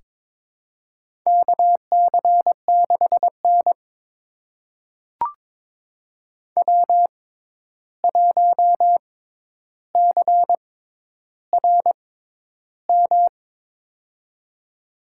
Morse code, heard as a keyed tone, about 22 wpm, spaced as for 11 wpm, 700 Hz. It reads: KC6N W1CRM
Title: USA Call Signs - 6x Character Spacing - 22wpm